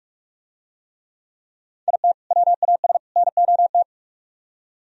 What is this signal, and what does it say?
Morse code, heard as a keyed tone, about 43 wpm, 700 Hz. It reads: IT WAS NOT